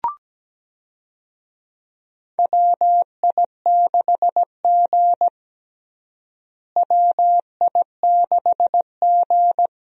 Morse code, heard as a keyed tone, about 17 wpm, 700 Hz. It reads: WI6G WI6G
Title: USA Call Signs - 4 Characters - Repeated 3x - 17wpm